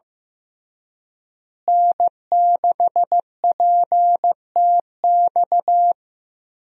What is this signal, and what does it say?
Morse code, heard as a keyed tone, about 15 wpm, 700 Hz. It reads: N6PTX
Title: USA Call Signs - 5 Characters - Repeated 3x - 15wpm